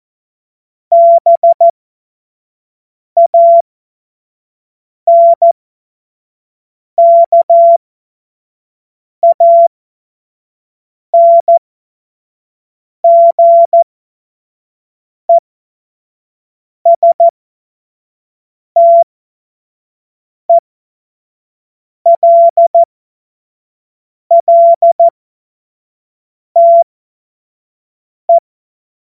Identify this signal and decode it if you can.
Morse code, heard as a keyed tone, about 14 wpm, spaced as for 5 wpm, 680 Hz. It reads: BANKANGESTELLTE
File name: Morsekurs / Witze / Witz 20 Mai 2025 25 F70